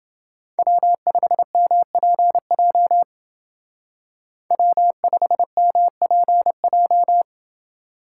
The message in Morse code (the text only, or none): W5MPJ W5MPJ